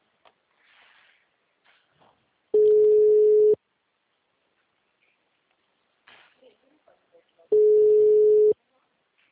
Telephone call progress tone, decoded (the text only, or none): ringback tone